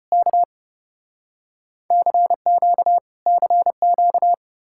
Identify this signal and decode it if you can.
Morse code, heard as a keyed tone, about 34 wpm, 700 Hz. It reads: K CQ CQ